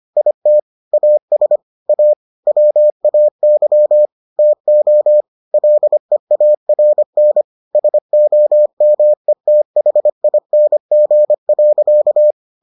Morse code, heard as a keyed tone, about 25 wpm, 600 Hz. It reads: IT AS A WAY TO LEARN SOMETHING.